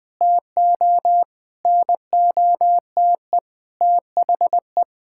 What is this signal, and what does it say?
Morse code, heard as a keyed tone, about 20 wpm, 700 Hz. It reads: TO NOTE THE
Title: Sentences with 5-Word Phrases - Spoken Beginning - 20wpm